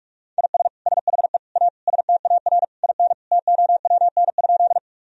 Morse code, heard as a keyed tone, about 42 wpm, 700 Hz. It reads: IS SHE A STAR IN TOWN?
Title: Sentences from Top 1000 Words - Review Part 2 - 45wpm